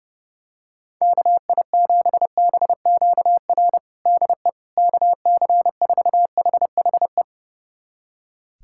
Morse code, heard as a keyed tone, about 30 wpm, 700 Hz. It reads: KI7BQR DE KC4HHE
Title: USA Callsign DE Callsign 30wpm